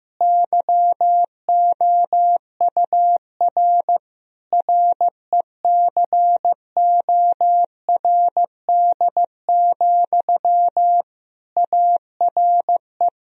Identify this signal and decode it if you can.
Morse code, heard as a keyed tone, about 16 wpm, 700 Hz. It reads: YOUR RECORD, ARE
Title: Sentences from Top 1000 Words - Review Part 3 - 15wpm